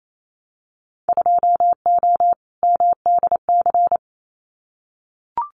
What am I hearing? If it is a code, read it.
Morse code, heard as a keyed tone, about 28 wpm, 700 Hz. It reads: 2O MDC
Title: Winter Field Day Exchanges 28wpm